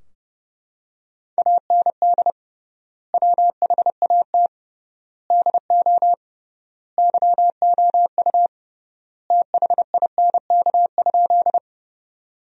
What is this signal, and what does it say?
Morse code, heard as a keyed tone, about 31 wpm, 700 Hz. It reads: AND WHAT DO YOU THINK?
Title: 3x Word-Spacing - Sentences from Top 200 Words - 30wpm